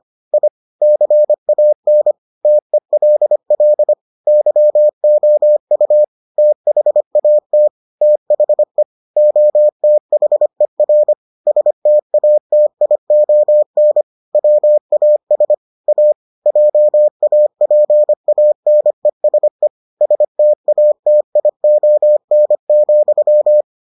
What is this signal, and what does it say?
Morse code, heard as a keyed tone, about 25 wpm, 600 Hz. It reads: I CAN TELL YOU THAT THE OTHER STATION WAS A JAPANESE STATION,